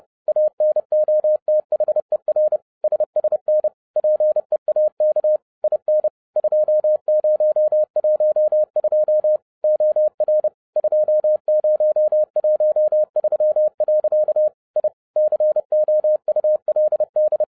Morse code, heard as a keyed tone, about 30 wpm, 610 Hz. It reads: ANOTHER SSN PEAK IN 2012 OR 2013. I COULD
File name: 2012-01-mp3 / n11-I DIDNT MISS THE SOLAR CYCLE 24-30wpm